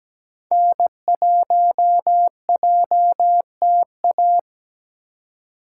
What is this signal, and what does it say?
Morse code, heard as a keyed tone, about 17 wpm, 700 Hz. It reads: N1JTA
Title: USA Call Signs - 5 Characters - 2x Character Spacing - 17wpm